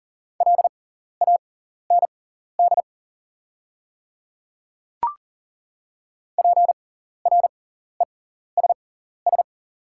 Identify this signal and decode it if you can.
Morse code, heard as a keyed tone, about 40 wpm, spaced as for 14 wpm, 700 Hz. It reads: LAND PRESS
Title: 6x Character Spacing - Top 300 Words - 40wpm